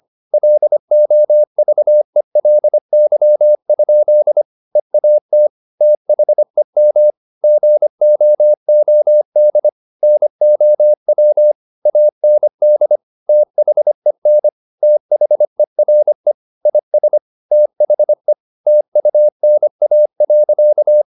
Morse code, heard as a keyed tone, about 25 wpm, 600 Hz. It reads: LOVELY? EAT THEM GOOD NOW AND THEN THERE IS THE TUNA.
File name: the old man and the sea part 1 25wpm